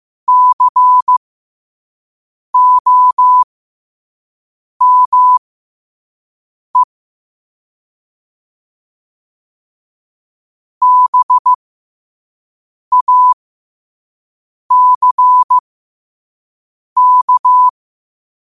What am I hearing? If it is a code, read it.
Morse code, heard as a keyed tone, about 15 wpm, spaced as for 5 wpm, 1000 Hz. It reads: COME BACK
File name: qotdfiles / 5wpm